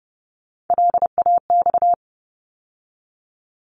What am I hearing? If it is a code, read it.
Morse code, heard as a keyed tone, about 30 wpm, 700 Hz. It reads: LAX
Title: ARRL Sections 30wpm